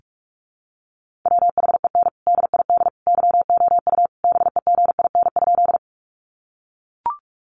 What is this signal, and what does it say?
Morse code, heard as a keyed tone, about 44 wpm, 700 Hz. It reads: WHEN DID YOU BEGIN?
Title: Sentences from Top 300 Words 45wpm